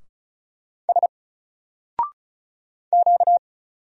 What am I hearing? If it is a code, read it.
Morse code, heard as a keyed tone, about 34 wpm, 700 Hz. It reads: S Q
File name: Single-Letter-Number-35wpm